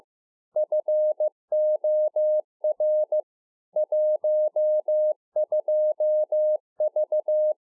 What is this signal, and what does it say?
Morse code, heard as a keyed tone, about 15 wpm, 610 Hz. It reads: FOR 12V
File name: 2011-04-mp3 / A SMALL TRANCEIVER 2 15WPM